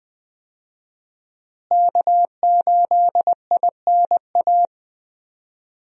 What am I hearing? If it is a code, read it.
Morse code, heard as a keyed tone, about 20 wpm, 700 Hz. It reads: K8INA
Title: USA Call Signs - 5 Characters - Repeated 2x - 20wpm